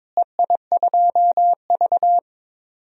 Morse code, heard as a keyed tone, about 21 wpm, 700 Hz. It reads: EI2V
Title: DXCC-100 Call Signs - 4 Characters - Repeated 2x - 22wpm